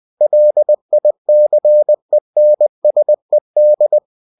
Morse code, heard as a keyed tone, about 20 wpm, 600 Hz. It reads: LICENSED
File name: cw-books / winnie-the-pooh / winnie-20wpm-page-0121